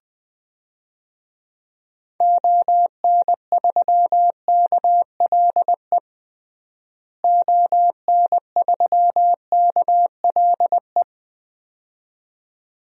Morse code, heard as a keyed tone, about 20 wpm, 700 Hz. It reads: ON3KLE ON3KLE
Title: DXCC-100 Call Signs - 6 Characters - Repeated 2x - 20wpm